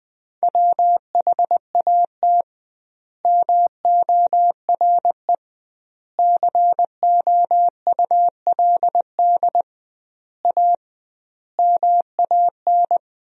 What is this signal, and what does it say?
Morse code, heard as a keyed tone, about 20 wpm, 700 Hz. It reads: WHAT MORE COULD A MAN
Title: 2x Word-Spacing - Sentences from Top 200 Words - 20wpm